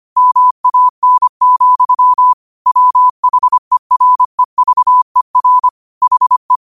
Morse code, heard as a keyed tone, about 25 wpm, 1000 Hz. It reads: MAN, WHEREVER HE
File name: qotdfiles / 25wpm